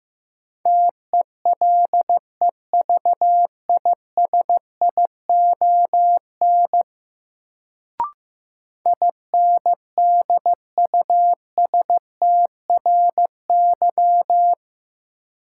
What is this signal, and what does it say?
Morse code, heard as a keyed tone, about 15 wpm, 700 Hz. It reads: TELEVISION INDUSTRY
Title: Sentences from Top 1000 Words - Review Part 4 - New Word Drill - 15wpm